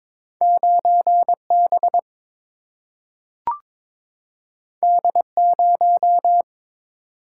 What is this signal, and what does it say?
Morse code, heard as a keyed tone, about 22 wpm, 700 Hz. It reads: 9 B D 0
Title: Single-Letter Single-Number 22wpm